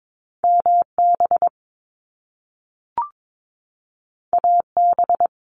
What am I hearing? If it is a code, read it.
Morse code, heard as a keyed tone, about 22 wpm, 700 Hz. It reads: MB AB
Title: Canadian Provinces and Territories 22wpm